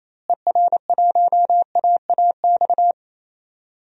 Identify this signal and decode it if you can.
Morse code, heard as a keyed tone, about 28 wpm, 700 Hz. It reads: ER1AAX
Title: DXCC-100 Call Signs - 6 Characters - Repeated 2x - 28wpm